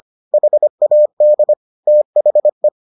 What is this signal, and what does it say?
Morse code, heard as a keyed tone, about 25 wpm, 600 Hz. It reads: HAD THE